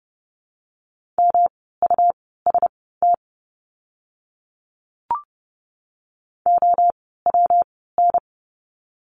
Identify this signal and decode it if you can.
Morse code, heard as a keyed tone, about 30 wpm, spaced as for 17 wpm, 700 Hz. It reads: MUST OWN
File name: course / 30:10wpm / 237 Lesson 38f - Letter q - Words 30wpm